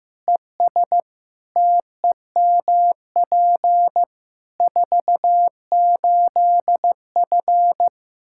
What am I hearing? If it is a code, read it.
Morse code, heard as a keyed tone, about 15 wpm, 700 Hz. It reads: ES TEMP 48F
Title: Practice QSO - Exchange 2 - Info - 15wpm